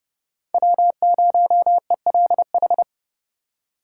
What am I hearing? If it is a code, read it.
Morse code, heard as a keyed tone, about 30 wpm, 700 Hz. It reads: W0ELH
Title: USA Call Signs - 5 Characters - 3x Character Spacing - 30wpm